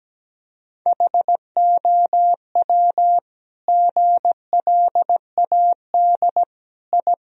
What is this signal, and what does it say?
Morse code, heard as a keyed tone, about 17 wpm, 700 Hz. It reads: HOW GLAD I
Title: Sentences from Top 1000 Words - Review Part 2 - 17wpm